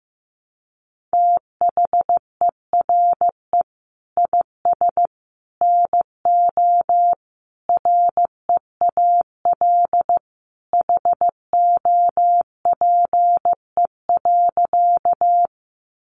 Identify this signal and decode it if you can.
Morse code, heard as a keyed tone, about 15 wpm, 700 Hz. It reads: THERE IS NO REAL HOPE.
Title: Sentences from Top 1000 Words 15wpm